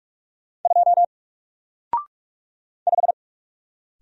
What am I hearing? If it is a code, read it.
Morse code, heard as a keyed tone, about 44 wpm, 700 Hz. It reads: 2 5
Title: Single Character (Letter:Number:Punctuation:Prosign) Alt 45wpm